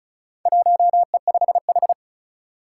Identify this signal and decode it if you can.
Morse code, heard as a keyed tone, about 36 wpm, 700 Hz. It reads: 1E5H
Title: Four Challenging Letter-Number Combinations 35wpm